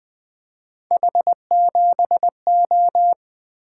Morse code, heard as a keyed tone, about 20 wpm, 700 Hz. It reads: H7O
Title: Three Letter-Number Combinations - 20wpm